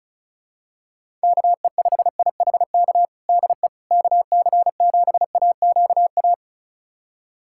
Morse code, heard as a keyed tone, about 36 wpm, 700 Hz. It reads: KE5IHK DE KC7AQA
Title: USA Callsign DE Callsign - 6 Characters - 35wpm